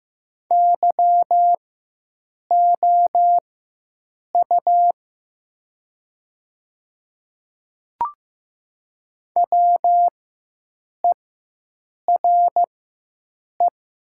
Morse code, heard as a keyed tone, about 15 wpm, spaced as for 7 wpm, 700 Hz. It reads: YOU WERE
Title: 4x Character Spacing - Top 100 Words - 15wpm